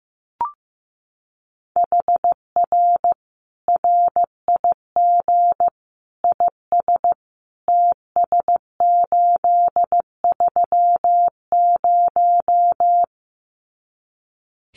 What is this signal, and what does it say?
Morse code, heard as a keyed tone, about 15 wpm, 700 Hz. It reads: HR RIG IS TS830
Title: Common QSO Phrases 15wpm